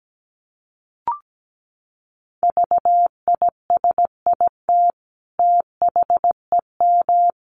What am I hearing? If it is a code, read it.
Morse code, heard as a keyed tone, about 17 wpm, 700 Hz. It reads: VISIT THEM
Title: Sentences from Top 900 Words 17wpm